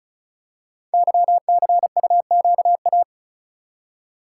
Morse code, heard as a keyed tone, about 36 wpm, 700 Hz. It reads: YCUQA
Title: Code Groups - 5 Letters - 35wpm